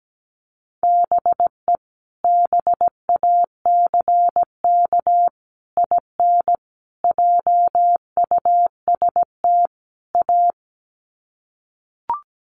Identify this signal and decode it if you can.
Morse code, heard as a keyed tone, about 17 wpm, 700 Hz. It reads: BE BACK IN JUST A
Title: Sets of 5 Words Encore 17wpm